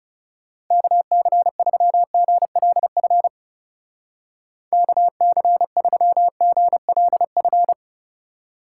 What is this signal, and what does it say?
Morse code, heard as a keyed tone, about 37 wpm, 700 Hz. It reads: KC3GLF KC3GLF
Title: Speed Racing - USA Call Signs 30wpm